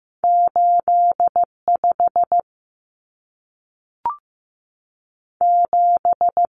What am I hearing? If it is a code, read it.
Morse code, heard as a keyed tone, about 15 wpm, 700 Hz. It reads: 85 7
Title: Common Cut Numbers - 2-Digit 15wpm